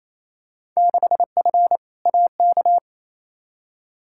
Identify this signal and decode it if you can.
Morse code, heard as a keyed tone, about 28 wpm, 700 Hz. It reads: 6F AK
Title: Field Day Exchanges Encore - 28wpm